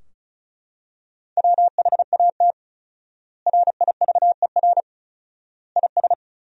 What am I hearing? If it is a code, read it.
Morse code, heard as a keyed tone, about 34 wpm, 700 Hz. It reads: WHAT RIVER IS
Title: 4x Word-Spacing - Sentences from Top 300 Words - 35wpm